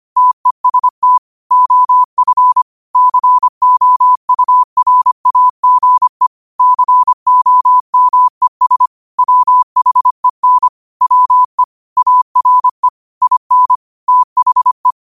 Morse code, heard as a keyed tone, about 25 wpm, 1000 Hz. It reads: TEST OF COURAGE COMES WHEN WE ARE IN THE
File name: qotdfiles / 25wpm